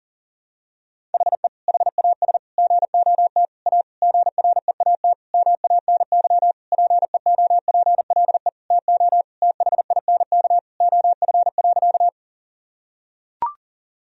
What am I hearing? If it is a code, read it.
Morse code, heard as a keyed tone, about 40 wpm, 700 Hz. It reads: HE HAS GOT A GREAT MANY PEOPLE TO THINK OF.
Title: Sentences from Top 1000 Words - Review Part 1 - 40wpm